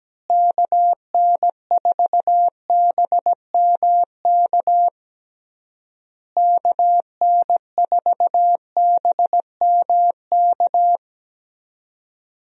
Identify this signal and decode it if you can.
Morse code, heard as a keyed tone, about 17 wpm, 700 Hz. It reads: KN4BMK KN4BMK
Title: USA Call Signs - 6 Characters - Repeated 3x - 17wpm